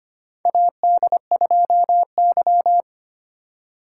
Morse code, heard as a keyed tone, about 25 wpm, 700 Hz. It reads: AD2Y
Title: USA Call Signs - 4x Character Spacing - 25wpm